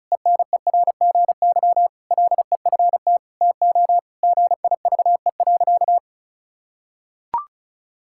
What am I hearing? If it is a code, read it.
Morse code, heard as a keyed tone, about 36 wpm, 700 Hz. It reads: ENERGY LEFT TO GIVE.
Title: Sentences from Top 1000 Words - Review Part 6 - 35wpm